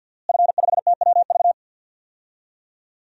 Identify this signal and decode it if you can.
Morse code, heard as a keyed tone, about 49 wpm, 700 Hz. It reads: F5TWV
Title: DXCC-100 Call Signs - 5 Characters - Repeated 3x - 50wpm